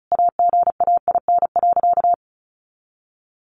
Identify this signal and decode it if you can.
Morse code, heard as a keyed tone, about 34 wpm, 700 Hz. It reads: AGAIN.
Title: Sentences from Top 1100 Words 35wpm